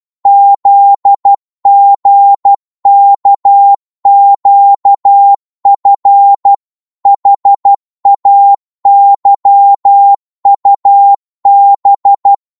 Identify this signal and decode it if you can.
Morse code, heard as a keyed tone, about 12 wpm, 800 Hz. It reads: ZGKQF HAYUB
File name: Morse / ltr-12wpm